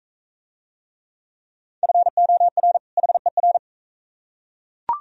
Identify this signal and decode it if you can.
Morse code, heard as a keyed tone, about 42 wpm, 700 Hz. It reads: FOR HER